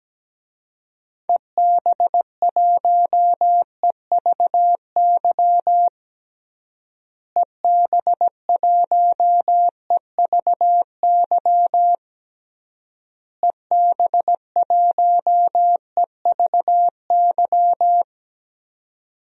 Morse code, heard as a keyed tone, about 17 wpm, 700 Hz. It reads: EB1EVY EB1EVY EB1EVY
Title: DXCC-100 Call Signs - 6 Characters - Repeated 3x - 17wpm